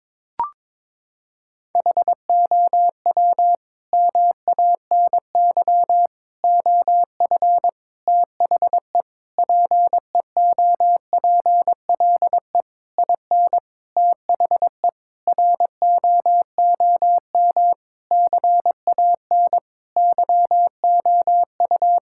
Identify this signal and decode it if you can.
Morse code, heard as a keyed tone, about 22 wpm, 700 Hz. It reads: HOW MANY OF THE PEOPLE IN THE ROOM CAN YOU
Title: Sentences from Top 1000 Words - Review Part 6 - 22wpm